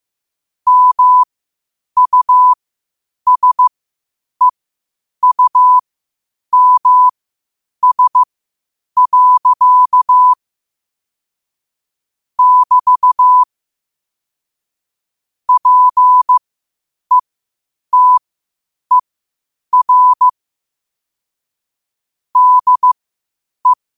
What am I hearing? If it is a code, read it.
Morse code, heard as a keyed tone, about 15 wpm, spaced as for 8 wpm, 1000 Hz. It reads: MUSEUMS. <BT> PETER DE
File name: qotdfiles / 8wpm